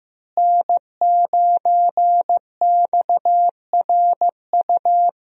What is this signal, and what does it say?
Morse code, heard as a keyed tone, about 15 wpm, 700 Hz. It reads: N9XRU
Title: USA Call Signs - 5 Characters - Repeated 3x - 15wpm